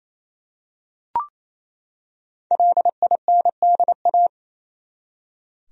Morse code, heard as a keyed tone, about 28 wpm, 700 Hz. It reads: LINDA
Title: USA Common Names - Max 5 Letters - 28wpm